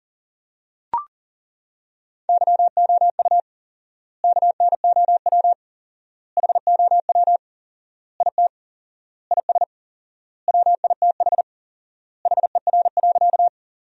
Morse code, heard as a keyed tone, about 40 wpm, 700 Hz. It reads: YOU KNOW HOW IT IS WITH HER.
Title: 4x Word-Spacing - Sentences from Top 100 Words - 40wpm